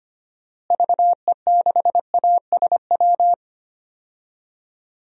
Morse code, heard as a keyed tone, about 25 wpm, 700 Hz. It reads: VE6ASW
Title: DXCC-100 Call Signs - 6 Characters - Repeated 2x - 25wpm